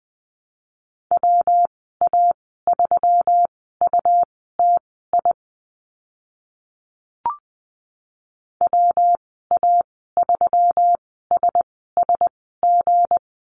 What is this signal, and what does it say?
Morse code, heard as a keyed tone, about 20 wpm, spaced as for 14 wpm, 700 Hz. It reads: WA3UTI WA3SSG
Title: 179 Lesson 29g - Slash - Call Signs 20wpm